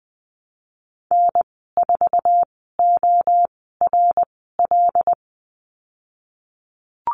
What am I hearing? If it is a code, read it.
Morse code, heard as a keyed tone, about 19 wpm, spaced as for 14 wpm, 700 Hz. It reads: N4ORL
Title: 154 Lesson 25g - Letter p - Call Signs 20wpm